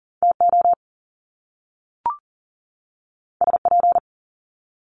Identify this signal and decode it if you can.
Morse code, heard as a keyed tone, about 40 wpm, 700 Hz. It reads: TO SP
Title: Top Two Letter Combinations in Words 40wpm